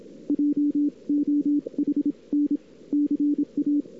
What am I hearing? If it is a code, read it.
Morse code, heard as a keyed tone, about 27 wpm, spaced as for 23 wpm, 290 Hz. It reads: JOHN CA
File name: cw-sprint-k5tr-op-n5rz